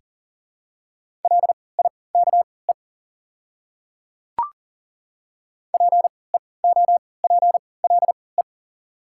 Morse code, heard as a keyed tone, about 40 wpm, spaced as for 23 wpm, 700 Hz. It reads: LIKE PEOPLE
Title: 3x Character Spacing - Top 100 Words - 40wpm